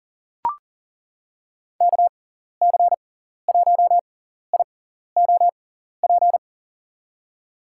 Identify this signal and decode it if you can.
Morse code, heard as a keyed tone, about 40 wpm, spaced as for 14 wpm, 700 Hz. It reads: KC1IOP
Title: USA Call Signs - 6 Characters - 6x Character Spacing - 40wpm